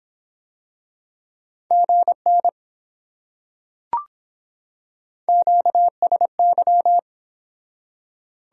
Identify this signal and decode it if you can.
Morse code, heard as a keyed tone, about 24 wpm, 700 Hz. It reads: GN QSY